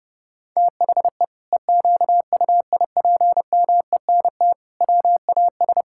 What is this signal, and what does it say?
Morse code, heard as a keyed tone, about 32 wpm, 700 Hz. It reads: THE EQUIPMENT WAS
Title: Sentences from Top 1000 Words - Review Part 5 - 30wpm